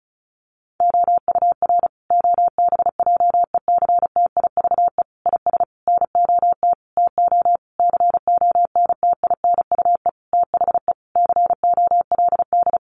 Morse code, heard as a keyed tone, about 34 wpm, 700 Hz. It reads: OUR OBJECTIVE IS NOT TO CONTINUE THE COLD
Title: Sentences from Top 1200 Words 35wpm